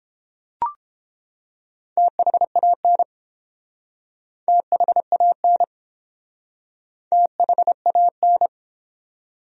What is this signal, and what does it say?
Morse code, heard as a keyed tone, about 30 wpm, 700 Hz. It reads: THAN THAN THAN